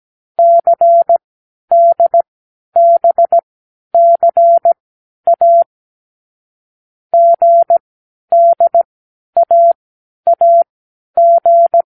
Morse code, heard as a keyed tone, about 17 wpm, spaced as for 10 wpm, 680 Hz. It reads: CDBCA GDAAG
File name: k7qo / Lesson12(A-H)